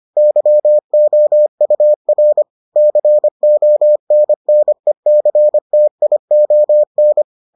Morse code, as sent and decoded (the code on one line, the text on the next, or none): -.-- --- ..- .-. / -.-. --- -. -. -.-. - .. --- -.
YOUR CONNECTION